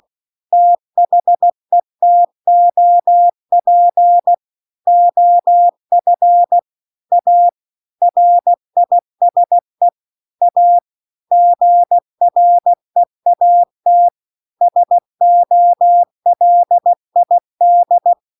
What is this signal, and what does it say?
Morse code, heard as a keyed tone, about 16 wpm, 700 Hz. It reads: THE TOP OF A RISE A GREAT SOLID